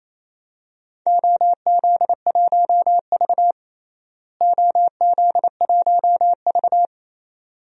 Morse code, heard as a keyed tone, about 28 wpm, 700 Hz. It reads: OZ1V OZ1V
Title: DXCC-100 Call Signs - 4 Characters - Repeated 2x - 28wpm